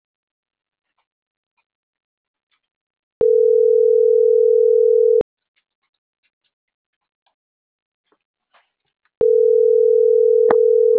Ringback tone (North America), 440 + 480 Hz.